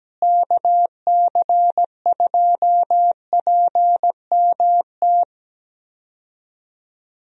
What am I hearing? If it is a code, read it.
Morse code, heard as a keyed tone, about 17 wpm, 700 Hz. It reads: KC2PMT
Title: USA Call Signs - 6 Characters - Repeated 2x - 17wpm